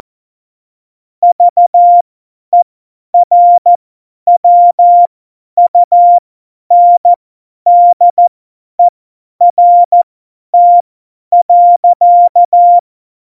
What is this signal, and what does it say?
Morse code, heard as a keyed tone, about 14 wpm, spaced as for 10 wpm, 700 Hz. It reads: VERWUNDERT.